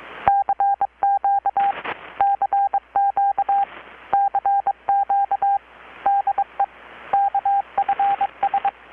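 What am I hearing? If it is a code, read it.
Morse code, heard as a keyed tone, about 22 wpm, 800 Hz, with radio noise. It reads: CQ CQ CQ DE KFS
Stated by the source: Morse from KPH's Night of Nights XX maritime radiotelegraphy event, received in Chiapas, Mexico about 3,800 km away on a Kenwood R-600 with a grounded 10 m long-wire antenna.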